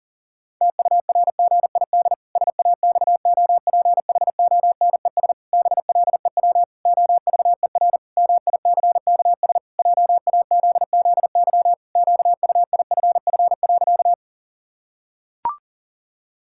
Morse code, heard as a keyed tone, about 40 wpm, 700 Hz. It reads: TURGID SAXOPHONES BLEW OVER MICKS JAZZY QUIFF.
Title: Pangram Challenge v2 - 40wpm